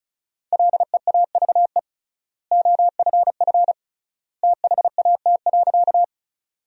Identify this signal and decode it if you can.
Morse code, heard as a keyed tone, about 36 wpm, 700 Hz. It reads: LEAVE OFF THAT.
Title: 3x Word-Spacing - Sentences from Top 400 Words - 35wpm